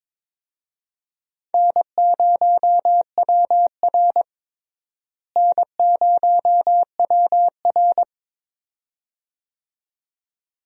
Morse code, heard as a keyed tone, about 22 wpm, 700 Hz. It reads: N0WR N0WR
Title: USA Call Signs - 4 Characters - Repeated 2x - 22wpm